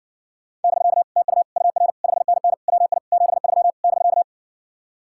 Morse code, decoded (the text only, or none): <BK> TU UR 5NN WI 73 <BK>